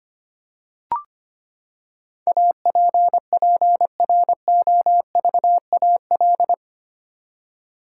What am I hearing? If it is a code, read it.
Morse code, heard as a keyed tone, about 25 wpm, 700 Hz. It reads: APPROVAL
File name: Eight Letter Words - 25wpm